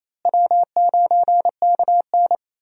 Morse code, heard as a keyed tone, about 28 wpm, 700 Hz. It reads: W9KN
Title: USA Call Signs - 4 Characters - Repeated 2x - 28wpm